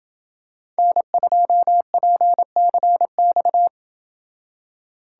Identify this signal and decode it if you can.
Morse code, heard as a keyed tone, about 28 wpm, 700 Hz. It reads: N2PCX